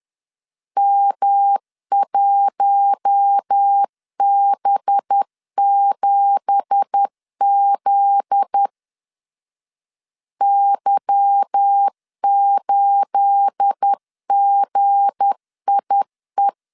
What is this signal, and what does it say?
Morse code, heard as a keyed tone, about 10 wpm, 800 Hz. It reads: M1B7Z Y8GIE
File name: CW / DET / 1 9 1500C S B 10wpm 4439